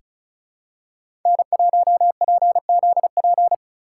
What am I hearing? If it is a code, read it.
Morse code, heard as a keyed tone, about 36 wpm, 700 Hz. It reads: N1PZP
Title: USA Call Signs - 5 Characters - 4x Character Spacing - 35wpm